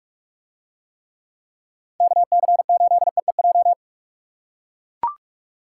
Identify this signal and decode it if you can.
Morse code, heard as a keyed tone, about 44 wpm, 700 Hz. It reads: KC8EEJ